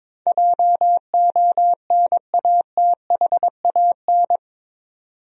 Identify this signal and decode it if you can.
Morse code, heard as a keyed tone, about 22 wpm, 700 Hz. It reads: JONATHAN
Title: USA Common Names - 22wpm